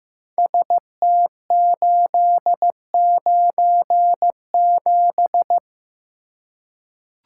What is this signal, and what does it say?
Morse code, heard as a keyed tone, about 15 wpm, 700 Hz. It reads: ST897
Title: Code Groups - 5 Letter:Numbers - 15wpm